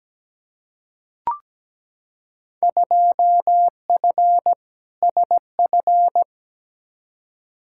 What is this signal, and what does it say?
Morse code, heard as a keyed tone, about 17 wpm, 700 Hz. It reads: 2F SF
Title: Field Day Exchanges - 17wpm